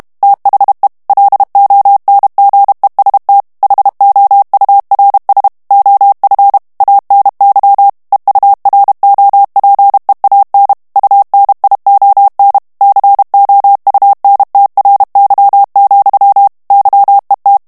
Morse code, heard as a keyed tone, about 32 wpm, 800 Hz. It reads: THE LONGEST HOURS OF ANY EUROPEAN UNION COUNTRY, YET
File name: morse mp3 / ANGLAIS texte clair 1 vitesse 30 mots minute